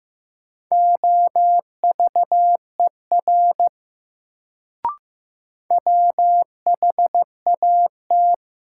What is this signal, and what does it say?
Morse code, heard as a keyed tone, about 15 wpm, 700 Hz. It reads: OVER WHAT
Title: Sentences from Top 100 Words - New Word Drill - 15wpm